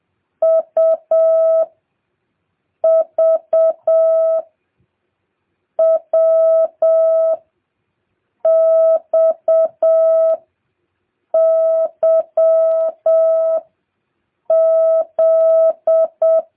Morse code, heard as a keyed tone, about 7 wpm, 660 Hz. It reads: U V W X Y Z